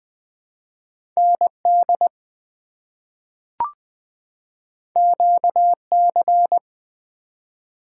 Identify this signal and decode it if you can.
Morse code, heard as a keyed tone, about 21 wpm, 700 Hz. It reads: ND QC
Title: ARRL Sections v2 - 20wpm